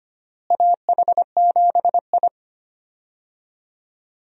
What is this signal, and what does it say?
Morse code, heard as a keyed tone, about 25 wpm, 700 Hz. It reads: AH7I
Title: USA Call Signs - 4 Characters - Repeated 3x - 25wpm